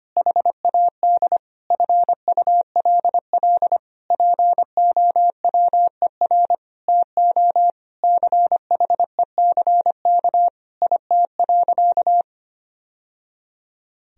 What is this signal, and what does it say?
Morse code, heard as a keyed tone, about 25 wpm, 700 Hz. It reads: HAD FULL POWER TO CHECK IT.
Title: Sentences from Top 1000 Words - Review Part 3 - 25wpm